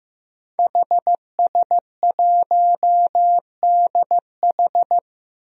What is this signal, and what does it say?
Morse code, heard as a keyed tone, about 15 wpm, 700 Hz. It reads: HS1DH
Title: DXCC-100 Call Signs - 5 Characters - Repeated 2x - 15wpm